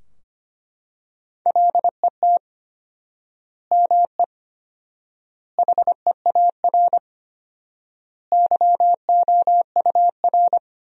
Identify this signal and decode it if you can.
Morse code, heard as a keyed tone, about 25 wpm, 700 Hz. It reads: LET ME HEAR YOUR
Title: 4x Word-Spacing - Sentences from Top 300 Words - 25wpm